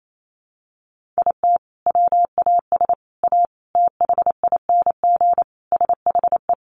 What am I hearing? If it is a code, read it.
Morse code, heard as a keyed tone, about 28 wpm, 700 Hz. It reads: IT WAS A THING SHE
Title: Sentences from Top 1400 Words 28wpm